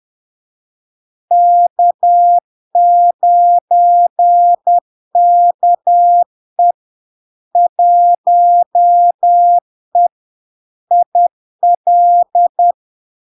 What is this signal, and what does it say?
Morse code, heard as a keyed tone, about 10 wpm, 700 Hz. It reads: K9KE 1E IL
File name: LARK-FD-2022-Log-in-CW-10wpm-700-Hz